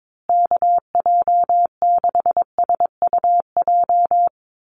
Morse code, heard as a keyed tone, about 22 wpm, 700 Hz. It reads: KJ6SUJ
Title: USA Call Signs Repeated 2x 22wpm